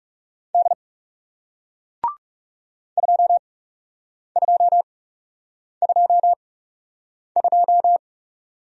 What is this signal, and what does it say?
Morse code, heard as a keyed tone, about 39 wpm, spaced as for 8 wpm, 700 Hz. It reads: D 2222